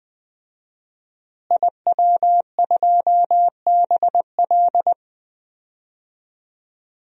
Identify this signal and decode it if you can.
Morse code, heard as a keyed tone, about 21 wpm, 700 Hz. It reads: IW2BL
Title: DXCC-100 Call Signs - 5 Characters - Repeated 3x - 20wpm